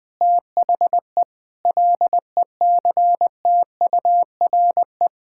Morse code, heard as a keyed tone, about 20 wpm, 700 Hz. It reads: THE LECTURE
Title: Sentences from Top 1600 Words 20wpm